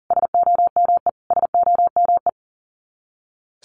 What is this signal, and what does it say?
Morse code, heard as a keyed tone, about 40 wpm, 700 Hz. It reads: SOME SOME
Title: Top 200 Words Repeated 40wpm